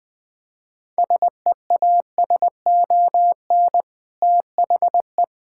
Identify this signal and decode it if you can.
Morse code, heard as a keyed tone, about 21 wpm, 700 Hz. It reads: SEASON THE
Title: Sentences with 5-Word Phrases - 20wpm